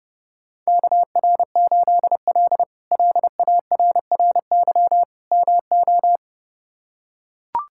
Morse code, heard as a keyed tone, about 30 wpm, 700 Hz. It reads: KR8L LARRY MO
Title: K1USN SST QSO Practice - 30wpm